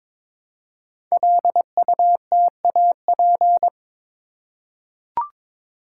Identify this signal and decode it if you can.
Morse code, heard as a keyed tone, about 21 wpm, 700 Hz. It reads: LUTAP
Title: Code Groups - 5 Letters - 22wpm